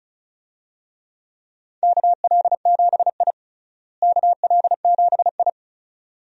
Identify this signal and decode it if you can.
Morse code, heard as a keyed tone, about 34 wpm, 700 Hz. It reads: KL7I KL7I
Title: DXCC-100 Call Signs - 4 Characters - Repeated 2x - 35wpm